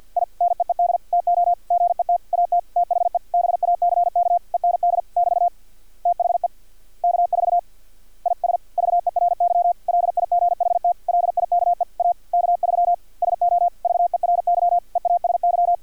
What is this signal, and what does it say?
Morse code, heard as a keyed tone, about 50 wpm, 700 Hz. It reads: I NEED TO GET AT THE BACK END <BT> THE K4 IS VERY LIGHT LIKE A K3 SO VERY EASY
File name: archived / CRSnet